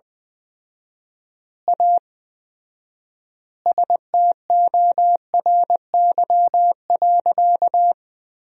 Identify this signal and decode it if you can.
Morse code, heard as a keyed tone, about 20 wpm, 700 Hz. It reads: A STORY.
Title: 4x Word-Spacing - Sentences from Top 300 Words - 20wpm